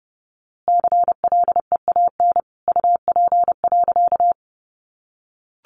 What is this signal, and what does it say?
Morse code, heard as a keyed tone, about 30 wpm, 700 Hz. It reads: CLEAN UP.
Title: A Sound Of Thunder 30wpm